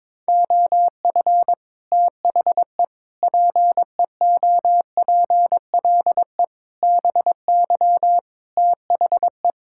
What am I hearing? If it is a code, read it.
Morse code, heard as a keyed tone, about 22 wpm, 700 Hz. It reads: OF THE PEOPLE BY THE
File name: Sets of 9 Words - 22wpm